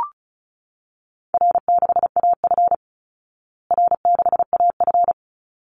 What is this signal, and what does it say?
Morse code, heard as a keyed tone, about 34 wpm, 700 Hz. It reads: R6AF R6AF
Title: Easy Call Signs Repeated 2x - 35wpm